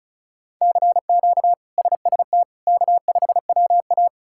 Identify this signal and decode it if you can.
Morse code, heard as a keyed tone, about 36 wpm, 700 Hz. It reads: CQ SST K5WA
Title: K1USN SST QSO Practice - 35wpm